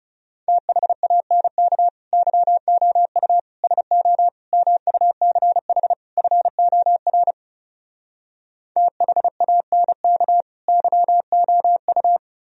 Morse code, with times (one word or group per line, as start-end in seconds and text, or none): 0.5-1.9 THANK
2.1-3.4 YOU
3.6-4.3 SO
4.5-5.9 MUCH
6.2-7.3 FOR
8.8-10.4 THANK
10.7-12.2 YOU